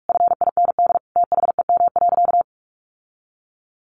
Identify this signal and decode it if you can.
Morse code, heard as a keyed tone, about 45 wpm, 700 Hz. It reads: FIND THEM.
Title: Sentences from Top 1500 Words 45wpm